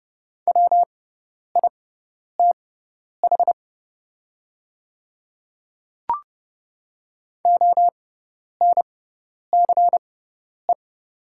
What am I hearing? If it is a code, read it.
Morse code, heard as a keyed tone, about 30 wpm, spaced as for 10 wpm, 700 Hz. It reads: WITH ONCE
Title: 6x Character Spacing - Top 300 Words - 30wpm